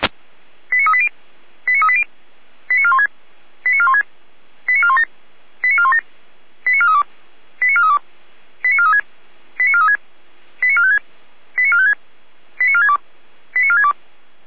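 Five-tone selective call (ZVEI), heard as quick runs of five tones, tones of about 70 ms. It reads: 89380 89380 89416 89416 89417 89417 89432 89432 89436 89436 89456 89456 89462 89462
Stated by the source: ZVEI five-tone selective call received by SDR.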